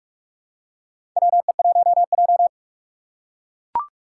Morse code, heard as a keyed tone, about 44 wpm, 700 Hz. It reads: WE1J
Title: USA Call Signs - 2x Character Spacing - 45wpm